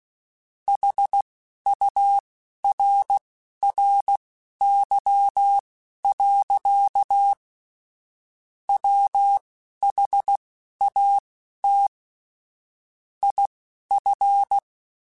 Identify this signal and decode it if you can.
Morse code, heard as a keyed tone, about 16 wpm, spaced as for 11 wpm, 800 Hz. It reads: HURRY. WHAT IF
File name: Technical Subjects / Technical Subjects-mp3s / AND Gate @10